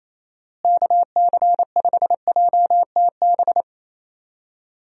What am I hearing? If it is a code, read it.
Morse code, heard as a keyed tone, about 28 wpm, 700 Hz. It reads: KC5JTB
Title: USA Call Signs - 6 Characters - Repeated 3x - 28wpm